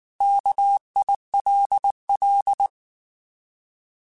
Morse code, heard as a keyed tone, about 19 wpm, 800 Hz. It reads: KILL